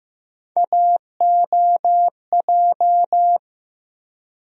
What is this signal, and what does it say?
Morse code, heard as a keyed tone, about 15 wpm, 700 Hz. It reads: AOJ
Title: USA Call Sign Postfixes - 3 Letters 15wpm